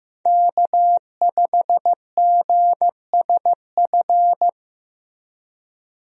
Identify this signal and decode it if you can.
Morse code, heard as a keyed tone, about 15 wpm, 700 Hz. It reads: K5GSF
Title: USA Call Signs - 5 Characters - Repeated 3x - 15wpm